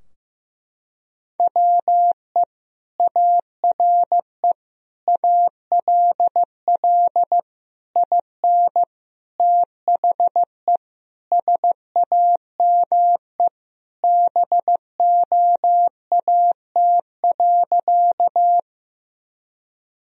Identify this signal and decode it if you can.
Morse code, heard as a keyed tone, about 16 wpm, 700 Hz. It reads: WE ARE ALL IN THE SAME BOAT.
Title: Sentences from Top 1000 Words - Review Part 3 - 15wpm